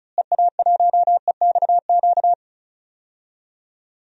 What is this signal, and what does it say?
Morse code, heard as a keyed tone, about 36 wpm, 700 Hz. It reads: EA1EXQ
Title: DXCC-100 Call Signs - 6 Characters - Repeated 2x - 35wpm